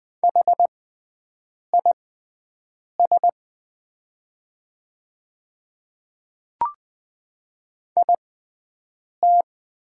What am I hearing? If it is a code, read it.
Morse code, heard as a keyed tone, about 20 wpm, spaced as for 7 wpm, 700 Hz. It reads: HIS IT
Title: 6x Character Spacing - Top 200 Words - 20wpm